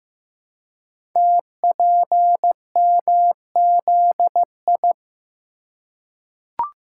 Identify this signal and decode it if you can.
Morse code, heard as a keyed tone, about 15 wpm, 700 Hz. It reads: TPMZI
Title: Code Groups - 5 Letters - 15wpm